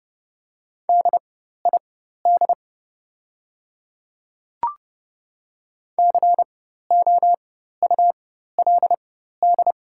Morse code, heard as a keyed tone, about 30 wpm, spaced as for 14 wpm, 700 Hz. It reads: DID COULD
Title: 4x Character Spacing - Top 200 Words - 30wpm